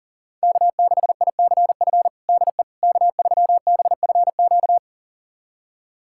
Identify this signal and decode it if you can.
Morse code, heard as a keyed tone, about 40 wpm, 700 Hz. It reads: K6ICF DE K3BFQ
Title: USA Callsign DE Callsign - 5 Characters - 40wpm